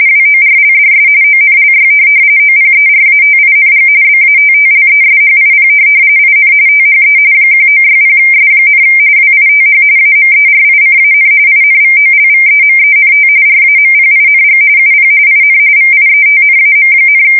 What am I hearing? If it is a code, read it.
RTTY, about 45.45 baud. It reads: WORLDS ONLY OPERATING BATTLESHIP IN THE 1960S, WITH AMERICAN INVOLVEMENT RAMPING UP IN VIETNAM,
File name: fsk170 / USS-New-Jersey